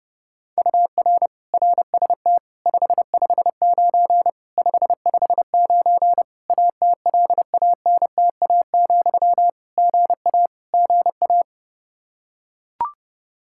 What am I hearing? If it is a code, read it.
Morse code, heard as a keyed tone, about 30 wpm, 700 Hz. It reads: UR RST 559 559 ATLANTA, GA GA
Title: K1USN SST QSO Practice 30wpm